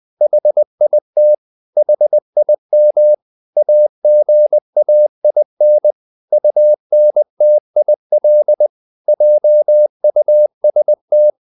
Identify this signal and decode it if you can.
Morse code, heard as a keyed tone, about 20 wpm, 600 Hz. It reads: HIT HIM AGAIN UNTIL JUST
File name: the old man and the sea part 2 20wpm